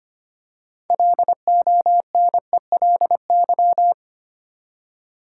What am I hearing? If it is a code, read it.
Morse code, heard as a keyed tone, about 25 wpm, 700 Hz. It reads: LONELY
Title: Six Letter Words - 25wpm